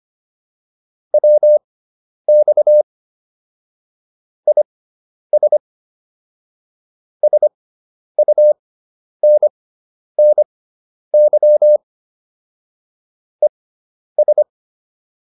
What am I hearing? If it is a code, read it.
Morse code, heard as a keyed tone, about 25 wpm, spaced as for 10 wpm, 600 Hz. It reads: WX IS SUNNY ES